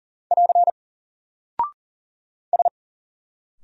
Morse code, heard as a keyed tone, about 40 wpm, 700 Hz. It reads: <AR> S
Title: Single Character (Letter:Number:Punctuation:Prosign) Alt 40wpm